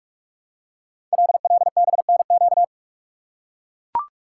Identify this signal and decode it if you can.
Morse code, heard as a keyed tone, about 44 wpm, 690 Hz. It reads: LLBNQ
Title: Code Groups - 5 Letters - 45wpm